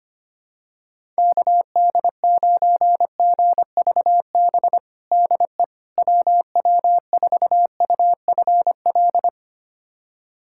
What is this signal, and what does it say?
Morse code, heard as a keyed tone, about 25 wpm, 700 Hz. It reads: KD9GVB DE WW4UFL
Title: USA Callsign DE Callsign - 6 Characters - 25wpm